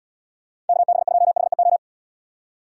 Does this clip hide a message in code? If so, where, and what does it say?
Yes: Morse code from 0.7 to 1.8 s, heard as a keyed tone, about 75 wpm, 700 Hz. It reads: KB2HEO